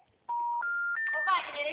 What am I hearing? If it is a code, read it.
Special information tone, with speech.